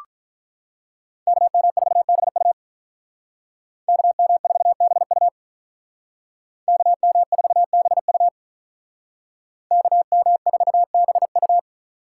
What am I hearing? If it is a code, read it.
Morse code, heard as a keyed tone, about 45 wpm, 700 Hz. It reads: KM4BU KM4BU KM4BU KM4BU